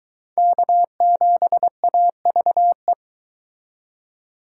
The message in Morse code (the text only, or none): K7AVE